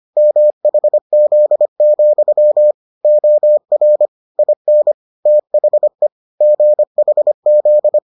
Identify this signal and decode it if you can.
Morse code, heard as a keyed tone, about 25 wpm, 600 Hz. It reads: MHZ, OR IN THE GHZ